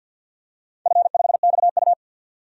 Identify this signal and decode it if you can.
Morse code, heard as a keyed tone, about 51 wpm, 700 Hz. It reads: F5XU
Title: DXCC-100 Call Signs - 4 Characters - 50wpm